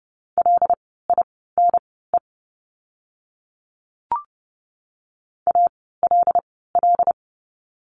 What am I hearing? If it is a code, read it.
Morse code, heard as a keyed tone, about 30 wpm, spaced as for 17 wpm, 700 Hz. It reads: LINE ALL
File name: course / 30:10wpm / 115 Lesson 19f - Letter w - Words 30wpm